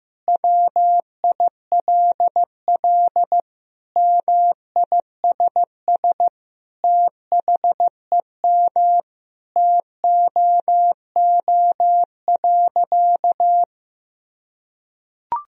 Morse code, heard as a keyed tone, about 15 wpm, 700 Hz. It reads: WILL MISS THEM TOO.
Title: Sentences from Top 1000 Words - Review Part 3 - 15wpm